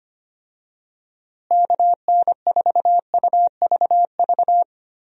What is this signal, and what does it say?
Morse code, heard as a keyed tone, about 25 wpm, 700 Hz. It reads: KN4UVV
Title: USA Call Signs - 6 Characters - Repeated 2x - 25wpm